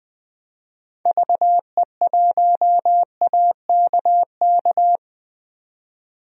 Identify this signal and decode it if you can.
Morse code, heard as a keyed tone, about 20 wpm, 700 Hz. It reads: VE1AKK
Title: DXCC-100 Call Signs - 6 Characters - 20wpm